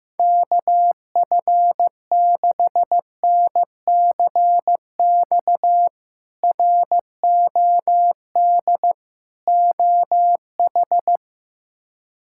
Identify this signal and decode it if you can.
Morse code, heard as a keyed tone, about 15 wpm, 700 Hz. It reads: KF6NCX ROD OH
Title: K1USN SST QSO Practice - 15wpm